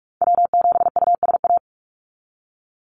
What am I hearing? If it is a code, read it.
Morse code, heard as a keyed tone, about 44 wpm, 700 Hz. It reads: W7USA
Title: NA SOTA Chaser Callsigns 45wpm